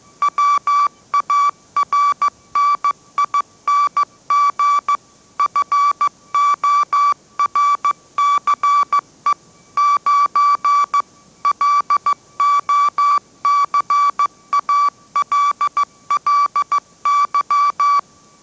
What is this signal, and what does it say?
Morse code, heard as a keyed tone, about 15 wpm, 1150 Hz. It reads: WARNING FORCE 9 LOCALLY
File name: mrx / mrx066 16wpm